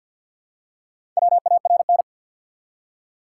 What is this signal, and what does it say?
Morse code, heard as a keyed tone, about 54 wpm, 700 Hz. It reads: WARN
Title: Four Letter Words - 50wpm